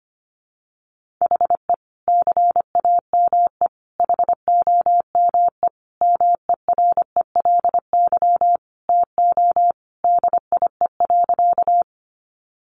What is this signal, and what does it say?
Morse code, heard as a keyed tone, about 25 wpm, 700 Hz. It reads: HE CAME HOME MERELY TO DIE.
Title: Sentences from Top 1400 Words 25wpm